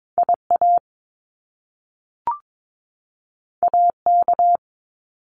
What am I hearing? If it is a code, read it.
Morse code, heard as a keyed tone, about 22 wpm, 700 Hz. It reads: IA AK
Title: US States 22wpm